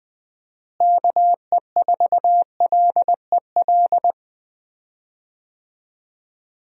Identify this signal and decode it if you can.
Morse code, heard as a keyed tone, about 20 wpm, 700 Hz. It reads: KE4LEL
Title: USA Call Signs - 6 Characters - Repeated 3x - 20wpm